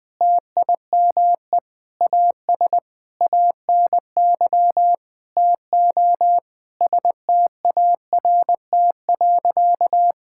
Morse code, heard as a keyed tone, about 20 wpm, 700 Hz. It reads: TIME AS ANY TO START.
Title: Sentences with 8-Word Phrases - 20wpm